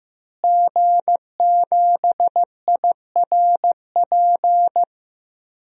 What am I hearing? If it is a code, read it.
Morse code, heard as a keyed tone, about 15 wpm, 700 Hz. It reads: G7IRP
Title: DXCC-100 Call Signs - 5 Characters - Repeated 3x - 15wpm